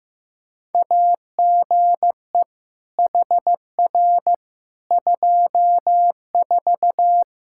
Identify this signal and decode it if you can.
Morse code, heard as a keyed tone, about 15 wpm, 700 Hz. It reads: AGE HR 24
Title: Practice QSO - Exchange 3 - More Info - 15wpm